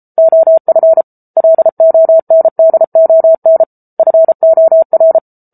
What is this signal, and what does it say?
Morse code, heard as a keyed tone, about 34 wpm, 650 Hz. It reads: OF LONDON FOR